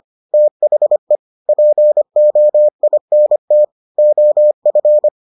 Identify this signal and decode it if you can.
Morse code, heard as a keyed tone, about 25 wpm, 600 Hz. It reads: THE POINT OF